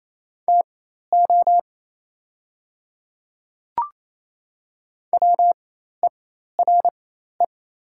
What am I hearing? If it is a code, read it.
Morse code, heard as a keyed tone, about 28 wpm, spaced as for 13 wpm, 690 Hz. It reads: TO WERE
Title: 4x Character Spacing - Top 200 Words - 28wpm